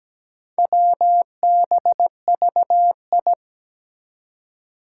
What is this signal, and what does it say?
Morse code, heard as a keyed tone, about 17 wpm, 700 Hz. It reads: WBVI
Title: Four Challenging Letter-Number Combinations 17wpm